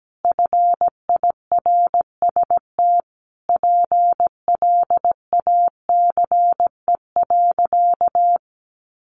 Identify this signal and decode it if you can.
Morse code, heard as a keyed tone, about 17 wpm, 700 Hz. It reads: FIRST PLACE.
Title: Additional Easy Phrases 17wpm